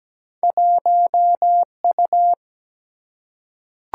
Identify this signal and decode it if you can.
Morse code, heard as a keyed tone, about 17 wpm, 700 Hz. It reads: 1U
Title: Single-Letter Single-Number 17wpm